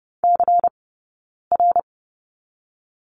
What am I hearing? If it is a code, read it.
Morse code, heard as a keyed tone, about 30 wpm, 700 Hz. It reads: C R
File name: Lesson 16d - Letter c - Two Characters 30wpm